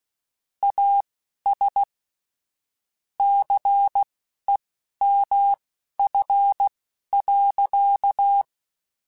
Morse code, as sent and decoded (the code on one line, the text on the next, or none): .- ... / -.-. -- ..-. .-.-.-
AS CEMF.